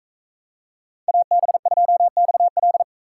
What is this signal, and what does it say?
Morse code, heard as a keyed tone, about 41 wpm, 700 Hz. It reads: AB2XL